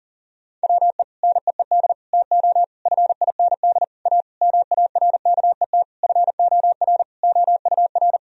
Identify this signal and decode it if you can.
Morse code, heard as a keyed tone, about 40 wpm, 700 Hz. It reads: WE NEED TO FIND A MARKET FOR OUR
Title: Sentences from Top 1000 Words - Review Part 3 - 40wpm